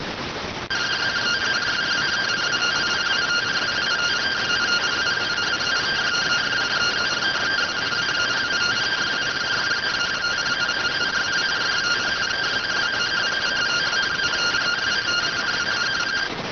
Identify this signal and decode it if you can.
Two-tone teleprinter signal, about 170 Hz shift.